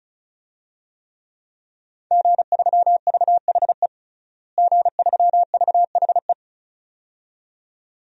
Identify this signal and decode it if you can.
Morse code, heard as a keyed tone, about 36 wpm, 700 Hz. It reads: G3VHE G3VHE
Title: DXCC-100 Call Signs - 5 Characters - Repeated 2x - 35wpm